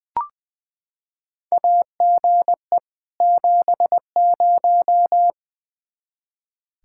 Morse code, heard as a keyed tone, about 20 wpm, 700 Hz. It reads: AGE 70
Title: Practice QSO - Exchange 3 - More Info - 20wpm